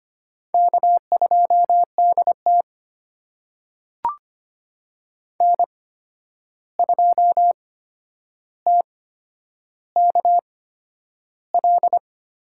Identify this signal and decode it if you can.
Morse code, heard as a keyed tone, about 25 wpm, spaced as for 10 wpm, 700 Hz. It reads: K2DT N2TKL
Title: USA Call Signs - 8x Character Spacing - 25wpm